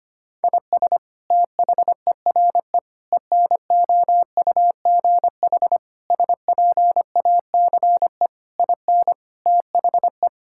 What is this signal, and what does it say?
Morse code, heard as a keyed tone, about 25 wpm, 700 Hz. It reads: IS THERE ENOUGH SPACE IN THE
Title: Sentences from Top 1000 Words - Review Part 5 - 25wpm